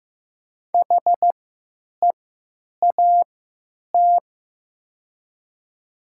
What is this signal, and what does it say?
Morse code, heard as a keyed tone, about 15 wpm, spaced as for 8 wpm, 700 Hz. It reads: HEAT
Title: 3x Character Spacing - Top 500 Words - 15wpm